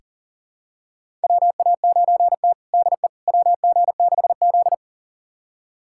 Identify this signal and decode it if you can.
Morse code, heard as a keyed tone, about 37 wpm, 700 Hz. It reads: WA9T DE WG6Z
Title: USA Callsign DE Callsign - 4 Characters - 40wpm